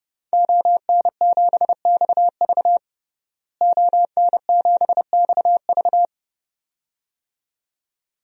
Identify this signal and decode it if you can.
Morse code, heard as a keyed tone, about 32 wpm, 700 Hz. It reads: ON7XV ON7XV
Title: DXCC-100 Call Signs - 5 Characters - Repeated 2x - 30wpm